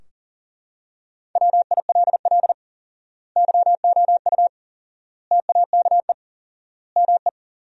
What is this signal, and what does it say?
Morse code, heard as a keyed tone, about 37 wpm, 700 Hz. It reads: WILL YOU TAKE ME
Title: 4x Word-Spacing - Sentences from Top 200 Words - 40wpm